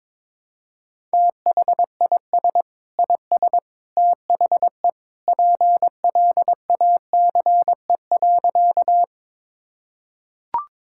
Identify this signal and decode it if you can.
Morse code, heard as a keyed tone, about 22 wpm, 700 Hz. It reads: THIS IS THE PLACE.
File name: practice / 22wpm / Sentences with 3-Word Phrases - 22wpm